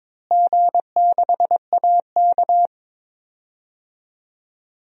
Morse code, heard as a keyed tone, about 21 wpm, 700 Hz. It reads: G6AK
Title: DXCC-100 Call Signs - 4 Characters - Repeated 2x - 22wpm